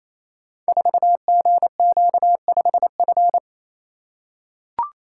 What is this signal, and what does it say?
Morse code, heard as a keyed tone, about 28 wpm, 700 Hz. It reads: 4GQ5F
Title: Code Groups - 5 Letter:Numbers - 28wpm